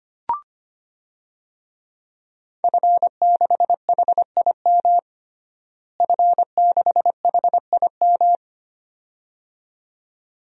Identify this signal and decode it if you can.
Morse code, heard as a keyed tone, about 25 wpm, 700 Hz. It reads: F6HIM F6HIM
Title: DXCC-100 Call Signs - 5 Characters - Repeated 2x - 25wpm